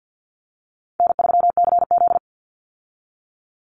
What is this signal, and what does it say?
Morse code, heard as a keyed tone, about 50 wpm, 700 Hz. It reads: N3CZ
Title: Call Signs Repeated 2x - 50wpm